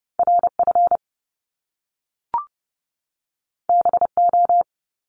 Morse code, heard as a keyed tone, about 30 wpm, 700 Hz. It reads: RF BO
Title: Top Two Letter Combinations in Words 30wpm